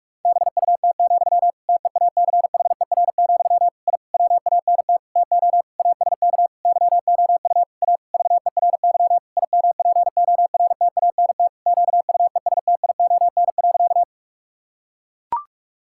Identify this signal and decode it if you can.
Morse code, heard as a keyed tone, about 41 wpm, 700 Hz. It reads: BUT, TEACHER, I WANT TO ASK YOU A VERY IMPORTANT QUESTION.
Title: Sentences from Top 1000 Words - Review Part 6 - 45wpm